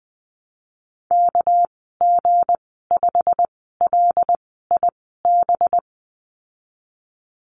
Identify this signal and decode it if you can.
Morse code, heard as a keyed tone, about 20 wpm, spaced as for 14 wpm, 700 Hz. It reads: KG5LIB
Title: 219 Lesson 35g - Number 0 - Call Signs 20wpm